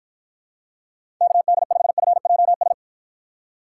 Morse code, heard as a keyed tone, about 54 wpm, 700 Hz. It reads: KD5FJS